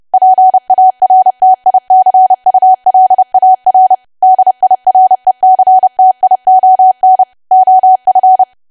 Morse code, heard as a keyed tone, about 30 wpm, 750 Hz. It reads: PARTICULAR DIRECTION OF
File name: cwa / W1AW-Level3 / 30WPM